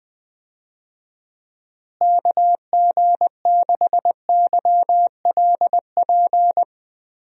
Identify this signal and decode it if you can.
Morse code, heard as a keyed tone, about 21 wpm, 700 Hz. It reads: KG6YLP
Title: USA Call Signs - 6 Characters - Repeated 3x - 20wpm